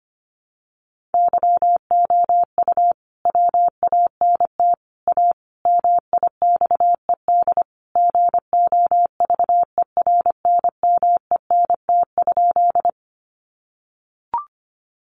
Morse code, heard as a keyed tone, about 25 wpm, 700 Hz. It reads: YOU WANT A MIXED GOVERNMENT?
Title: Sentences from Top 1500 Words 25wpm